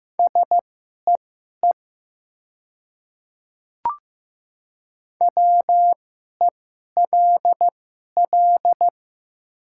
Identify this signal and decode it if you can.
Morse code, heard as a keyed tone, about 15 wpm, spaced as for 11 wpm, 700 Hz. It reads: SEE WELL
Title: 2x Character Spacing - Top 200 Words - 15wpm